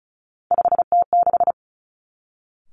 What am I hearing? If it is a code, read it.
Morse code, heard as a keyed tone, about 36 wpm, 700 Hz. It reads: 5T6
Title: Three Challenging Letter-Number Combinations 35wpm